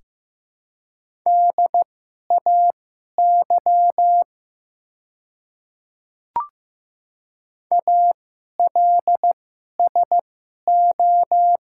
Morse code, heard as a keyed tone, about 15 wpm, spaced as for 11 wpm, 700 Hz. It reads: DAY ALSO
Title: 2x Character Spacing - Top 200 Words - 15wpm